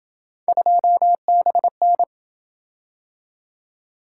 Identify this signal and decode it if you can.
Morse code, heard as a keyed tone, about 27 wpm, 700 Hz. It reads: 2BN